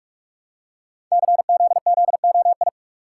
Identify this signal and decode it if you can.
Morse code, heard as a keyed tone, about 44 wpm, 700 Hz. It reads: CZZOI
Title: Code Groups - 5 Letters - 45wpm